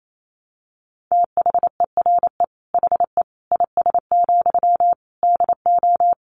Morse code, heard as a keyed tone, about 28 wpm, 700 Hz. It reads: THERE HE IS, DO
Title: Sentences from Top 100 Words 28wpm